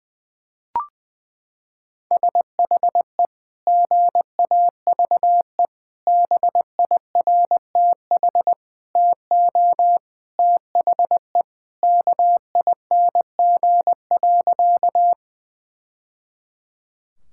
Morse code, heard as a keyed tone, about 21 wpm, 700 Hz. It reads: SHE GAVE BIRTH TO THE KING.
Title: Sentences from Top 1000 Words - Review Part 5 - 20wpm